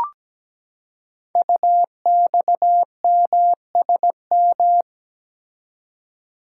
Morse code, heard as a keyed tone, about 17 wpm, 700 Hz. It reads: UXMSM
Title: Code Groups - 5 Letters - 17wpm